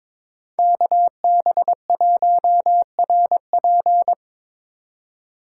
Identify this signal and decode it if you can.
Morse code, heard as a keyed tone, about 26 wpm, 700 Hz. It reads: KB1RP
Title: Speed Racing - USA 2x2 Call Signs 22wpm